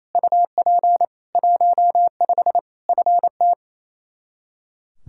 Morse code, heard as a keyed tone, about 28 wpm, 700 Hz. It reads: UP 15 FT
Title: Practice QSO - Exchange 2 - Info - 28wpm